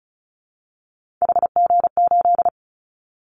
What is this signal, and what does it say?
Morse code, heard as a keyed tone, about 34 wpm, 700 Hz. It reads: HG8
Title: Callsign Prefixes for Top 25 DXCC - 35wpm